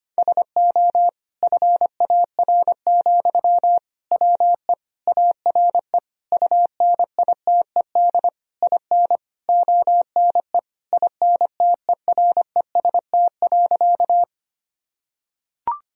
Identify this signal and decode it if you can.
Morse code, heard as a keyed tone, about 25 wpm, 700 Hz. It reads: SO FAR, WE ARE UNITED IN ONE INTEREST.
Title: Sentences from Top 1000 Words - Review Part 4 - 25wpm